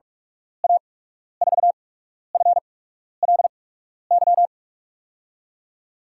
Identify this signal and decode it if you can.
Morse code, heard as a keyed tone, about 47 wpm, spaced as for 12 wpm, 700 Hz. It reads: A4FLY
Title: USA Call Signs - 6 Characters - 8x Character Spacing - 45wpm